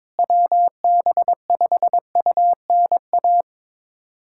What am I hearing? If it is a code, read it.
Morse code, heard as a keyed tone, about 22 wpm, 700 Hz. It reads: WB5UNA
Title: USA Callsign DE Callsign - 6 Characters - 22wpm